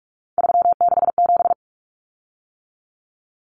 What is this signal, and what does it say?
Morse code, heard as a keyed tone, about 44 wpm, 700 Hz. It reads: 367
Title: Common Cut Numbers - 3-Digit 45wpm